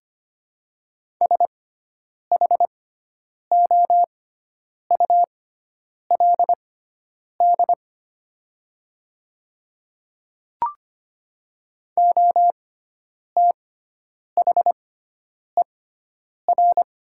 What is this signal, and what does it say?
Morse code, heard as a keyed tone, about 25 wpm, spaced as for 9 wpm, 700 Hz. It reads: SHOULD OTHER
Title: 6x Character Spacing - Top 300 Words - 25wpm